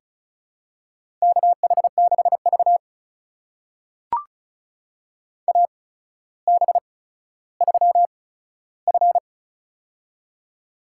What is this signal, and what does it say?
Morse code, heard as a keyed tone, about 36 wpm, spaced as for 15 wpm, 700 Hz. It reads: KH6V AB3F
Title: USA Call Signs - 4 Characters - 8x Character Spacing - 35wpm